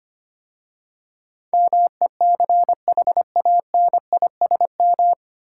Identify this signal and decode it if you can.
Morse code, heard as a keyed tone, about 25 wpm, 700 Hz. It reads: MECHANISM
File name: practice / 25wpm / 9 Letter Words - First 3 Letters Given - 25wpm